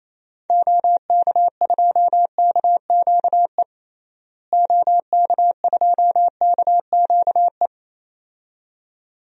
Morse code, heard as a keyed tone, about 28 wpm, 700 Hz. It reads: OK2KQE OK2KQE
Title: DXCC-100 Call Signs - 6 Characters - Repeated 2x - 28wpm